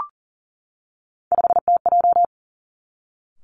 Morse code, heard as a keyed tone, about 39 wpm, 700 Hz. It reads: 5TJ
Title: Three Challenging Letter-Number Combinations 40wpm